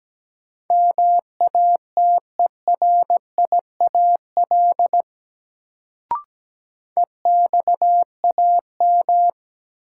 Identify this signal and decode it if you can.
Morse code, heard as a keyed tone, about 17 wpm, 700 Hz. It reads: MATERIAL EXAM
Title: Sentences from Top 1000 Words - Review Part 5 - New Word Drill - 17wpm